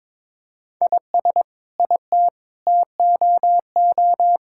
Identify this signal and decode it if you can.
Morse code, heard as a keyed tone, about 22 wpm, 700 Hz. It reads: IS IT TOO
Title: Sentences from Top 1600 Words 22wpm